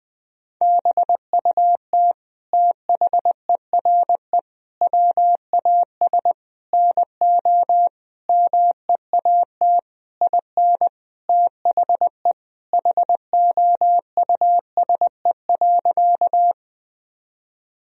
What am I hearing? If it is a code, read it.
Morse code, heard as a keyed tone, about 21 wpm, 700 Hz. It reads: BUT THERE WAS NO MEAT IN THE HOUSE.
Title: Sentences from Top 1000 Words - Review Part 3 - 20wpm